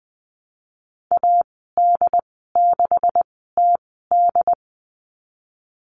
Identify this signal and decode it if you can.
Morse code, heard as a keyed tone, about 20 wpm, spaced as for 15 wpm, 700 Hz. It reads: AD6TD
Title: 186 Lesson 30g - Letter b - Call Signs 20wpm